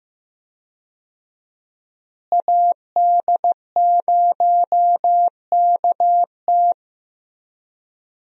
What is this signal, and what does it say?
Morse code, heard as a keyed tone, about 15 wpm, 700 Hz. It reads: AD0KT
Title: USA Call Signs - 5 Characters - Repeated 2x - 15wpm